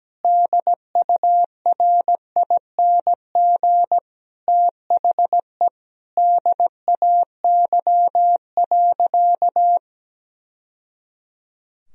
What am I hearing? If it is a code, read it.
Morse code, heard as a keyed tone, about 17 wpm, 700 Hz. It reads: DURING THE DAY.
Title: Sentences from Top 1000 Words - Review Part 2 - 17wpm